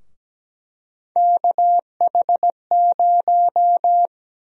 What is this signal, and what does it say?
Morse code, heard as a keyed tone, about 17 wpm, 700 Hz. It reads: KH0
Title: USA Call Sign Prefixes - 17wpm